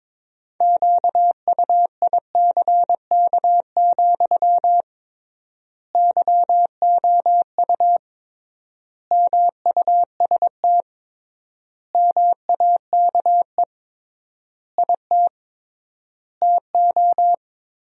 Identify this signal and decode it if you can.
Morse code, heard as a keyed tone, about 22 wpm, 700 Hz. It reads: QUICK, YOU MUST MAKE IT TO
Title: 3x Word-Spacing - Sentences from Top 500 Words - 22wpm